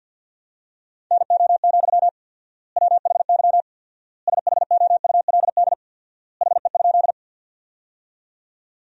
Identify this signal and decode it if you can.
Morse code, heard as a keyed tone, about 47 wpm, 700 Hz. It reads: NO, WHY SHOULD HE?
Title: 4x Word-Spacing - Sentences from Top 300 Words - 50wpm